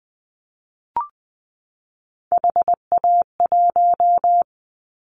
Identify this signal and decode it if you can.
Morse code, heard as a keyed tone, about 20 wpm, 700 Hz. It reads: HA1
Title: Callsign Prefixes for Top 50 DXCC - 20wpm